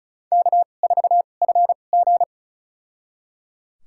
Morse code, heard as a keyed tone, about 36 wpm, spaced as for 25 wpm, 690 Hz. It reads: K4FG
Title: USA Call Signs - 4 Characters - 2x Character Spacing - 35wpm